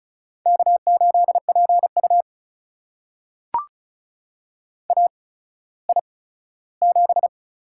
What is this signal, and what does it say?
Morse code, heard as a keyed tone, about 36 wpm, spaced as for 17 wpm, 700 Hz. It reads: K8PU AI7
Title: USA Call Signs - 4 Characters - 8x Character Spacing - 35wpm